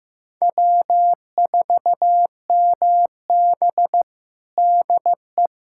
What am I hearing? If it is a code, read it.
Morse code, heard as a keyed tone, about 15 wpm, 700 Hz. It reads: W4MB DE
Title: USA Callsign DE Callsign - 4 Characters - 15wpm